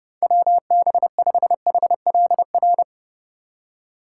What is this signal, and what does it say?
Morse code, heard as a keyed tone, about 40 wpm, 700 Hz. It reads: WB5HLR